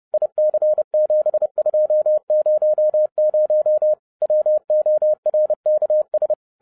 Morse code, heard as a keyed tone, about 30 wpm, 610 Hz. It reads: IC7200 WORKS
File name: 2012-01-mp3 / n11-I DIDNT MISS THE SOLAR CYCLE 24-30wpm